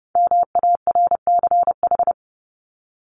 Morse code, heard as a keyed tone, about 30 wpm, 700 Hz. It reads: MARCH
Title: Top 1501-1750 Words 30wpm